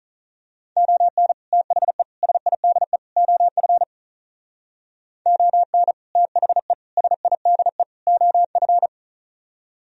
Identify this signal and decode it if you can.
Morse code, heard as a keyed tone, about 38 wpm, 700 Hz. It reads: ON THE SIDE OF ON THE SIDE OF
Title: Speed-Racing - Sets of 4 Words 35wpm